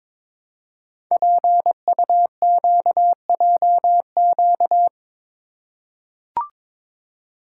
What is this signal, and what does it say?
Morse code, heard as a keyed tone, about 22 wpm, 700 Hz. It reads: PUQJQ
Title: Code Groups - 5 Letters - 22wpm